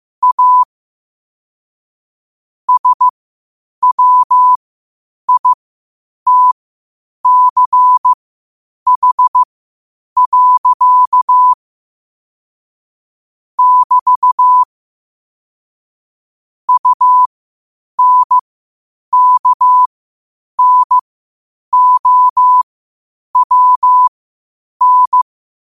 Morse code, heard as a keyed tone, about 15 wpm, spaced as for 8 wpm, 1000 Hz. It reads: A SWITCH. <BT> UNKNOWN
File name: qotdfiles / 8wpm